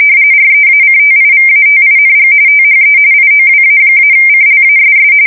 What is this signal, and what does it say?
RTTY, about 45.45 baud. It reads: REACTIVATING THE IOWA-CLASS
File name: fsk170 / USS-New-Jersey